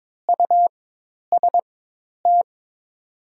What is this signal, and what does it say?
Morse code, heard as a keyed tone, about 22 wpm, spaced as for 10 wpm, 700 Hz. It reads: UST
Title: 4x Character Spacing - Top 500 Words - 22wpm